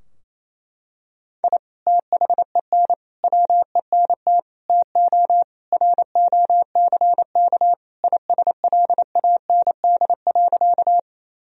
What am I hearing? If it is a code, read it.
Morse code, heard as a keyed tone, about 28 wpm, 700 Hz. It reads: I THEN WENT TO ROCK ISLAND.
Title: Sentences from Top 1000 Words - Review Part 3 - 28wpm